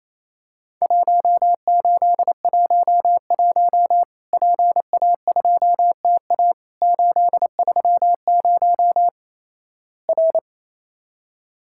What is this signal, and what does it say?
Morse code, heard as a keyed tone, about 28 wpm, 700 Hz. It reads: 1811 PA2TA 830 R
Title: Worked All Europe DX Contest - QTC Exchange - 28wpm